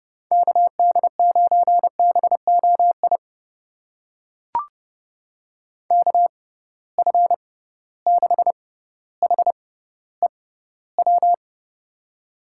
Morse code, heard as a keyed tone, about 30 wpm, spaced as for 15 wpm, 700 Hz. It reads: KD9BOI KF6HEW
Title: USA Call Signs - 6 Characters - 6x Character Spacing - 30wpm